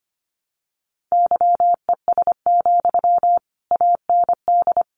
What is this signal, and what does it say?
Morse code, heard as a keyed tone, about 25 wpm, 700 Hz. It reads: YES, AND
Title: Sentences from Top 500 Words 25wpm